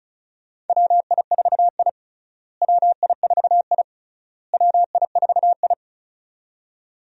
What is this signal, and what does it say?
Morse code, heard as a keyed tone, about 34 wpm, 700 Hz. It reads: WI4I WI4I WI4I
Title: USA Call Signs - 4 Characters - Repeated 3x - 35wpm